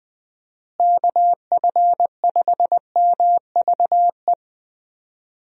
Morse code, heard as a keyed tone, about 20 wpm, 700 Hz. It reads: KF5MVE
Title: USA Call Signs - 6 Characters - Repeated 3x - 20wpm